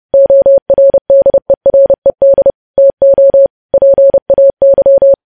Morse code, heard as a keyed tone, about 30 wpm, 550 Hz. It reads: ORDERED TO PAY
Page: news headlines in morse code at 30 wpm